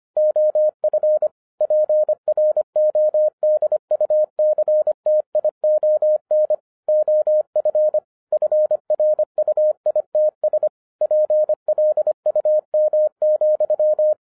Morse code, heard as a keyed tone, about 25 wpm, 610 Hz. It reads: OF PRODUCTION OF FRUITS PLUM,
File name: 2011-10-mp3 / n04-JUST MOVED 25wpm